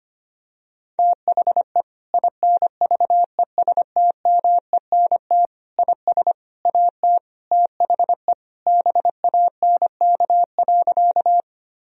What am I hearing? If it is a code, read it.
Morse code, heard as a keyed tone, about 25 wpm, 700 Hz. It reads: THE INVESTMENT IS AT THE BANK.
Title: Sentences from Top 1000 Words - Review Part 4 - 25wpm